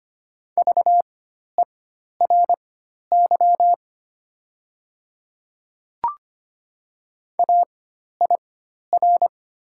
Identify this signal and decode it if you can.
Morse code, heard as a keyed tone, about 25 wpm, spaced as for 12 wpm, 700 Hz. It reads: VERY AIR
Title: 4x Character Spacing - Top 200 Words - 25wpm